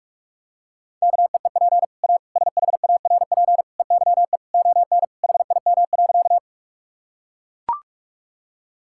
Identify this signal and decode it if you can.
Morse code, heard as a keyed tone, about 41 wpm, 700 Hz. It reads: KEEP A SHARP EYE ON HIM.
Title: Sentences from Top 1600 Words 45wpm